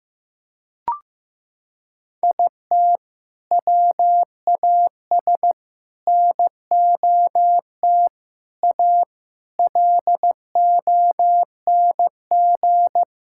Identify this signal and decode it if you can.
Morse code, heard as a keyed tone, about 14 wpm, 700 Hz. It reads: IT WAS NOT A LONG
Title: Sentences from Top 1000 Words - Review Part 6 - 15wpm